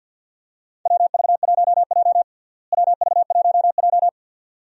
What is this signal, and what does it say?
Morse code, heard as a keyed tone, about 50 wpm, 700 Hz. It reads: WV1J WV1J
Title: USA Call Signs - 4 Characters - Repeated 2x - 50wpm